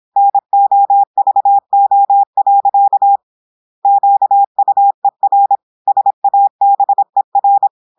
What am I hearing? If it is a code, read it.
Morse code, heard as a keyed tone, about 26 wpm, 800 Hz. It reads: NOVO. QUER SABER
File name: portugais texte en clair 2 vitesse 26 mots minutes